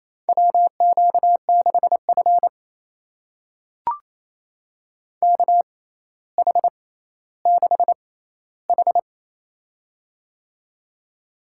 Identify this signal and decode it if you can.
Morse code, heard as a keyed tone, about 28 wpm, spaced as for 14 wpm, 700 Hz. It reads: WQ6F KH6H
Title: USA Call Signs - 4 Characters - 6x Character Spacing - 28wpm